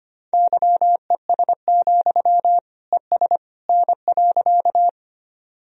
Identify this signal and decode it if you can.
Morse code, heard as a keyed tone, about 25 wpm, 700 Hz. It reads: YES, ES N.
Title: Sentences from Top 500 Words and Abbreviations 25wpm